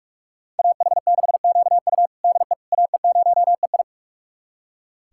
Morse code, heard as a keyed tone, about 46 wpm, 700 Hz. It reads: AH6QU DE AE0EI
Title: USA Callsign DE Callsign - 5 Characters - 45wpm